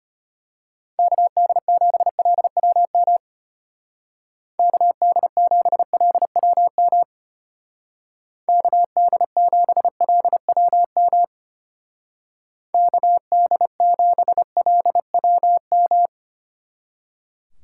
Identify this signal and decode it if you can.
Morse code, heard as a keyed tone, about 32 wpm, 700 Hz. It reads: KD7LWM KD7LWM KD7LWM KD7LWM